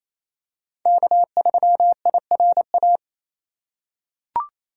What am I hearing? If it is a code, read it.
Morse code, heard as a keyed tone, about 28 wpm, 700 Hz. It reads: K3IRA
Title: USA Call Signs - 5 Characters - 2x Character Spacing - 28wpm